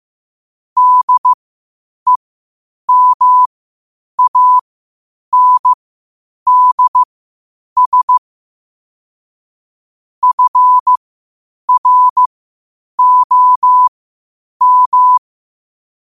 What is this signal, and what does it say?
Morse code, heard as a keyed tone, about 15 wpm, spaced as for 8 wpm, 1000 Hz. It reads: DEMANDS FROM
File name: qotdfiles / 8wpm